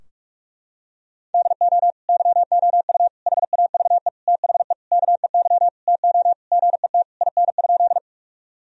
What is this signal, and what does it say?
Morse code, heard as a keyed tone, about 41 wpm, 700 Hz. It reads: DO YOU HAVE THE KEY TO GET IN?
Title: Sentences from Top 1000 Words - Review Part 4 - 45wpm